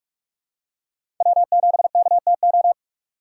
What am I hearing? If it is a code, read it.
Morse code, heard as a keyed tone, about 46 wpm, 700 Hz. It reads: W7KTO
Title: USA Call Signs - 6x Character Spacing - 45wpm